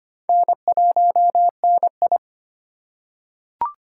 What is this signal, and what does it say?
Morse code, heard as a keyed tone, about 25 wpm, 700 Hz. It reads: N1NI
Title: USA Call Signs - 4 Characters - 2x Character Spacing - 25wpm